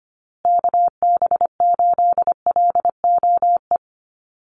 Morse code, heard as a keyed tone, about 25 wpm, 700 Hz. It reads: KB8LOE
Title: USA Call Signs Repeated 2x 25wpm